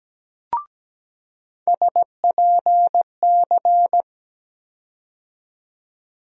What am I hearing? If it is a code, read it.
Morse code, heard as a keyed tone, about 17 wpm, 700 Hz. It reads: SPC
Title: USA Call Sign Postfixes - 3 Letters 17wpm